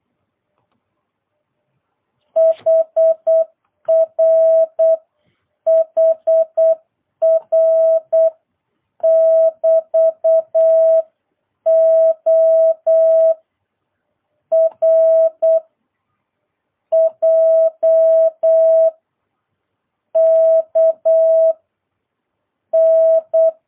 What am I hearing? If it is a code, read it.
Morse code, heard as a keyed tone, about 8 wpm, 660 Hz. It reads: HR HR <BT> O R J K N